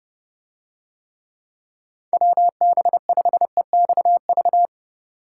Morse code, heard as a keyed tone, about 29 wpm, 700 Hz. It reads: WB5EXV